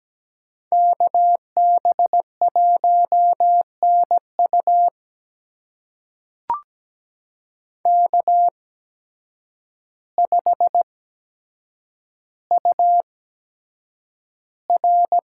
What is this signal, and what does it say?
Morse code, heard as a keyed tone, about 17 wpm, spaced as for 8 wpm, 700 Hz. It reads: KB1NU K5UR
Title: USA Call Signs - 5 Characters - 8x Character Spacing - 17wpm